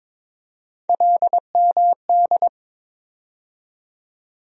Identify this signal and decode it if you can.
Morse code, heard as a keyed tone, about 22 wpm, 700 Hz. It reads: LMD
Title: USA Call Sign Postfixes - 3 Letters 22wpm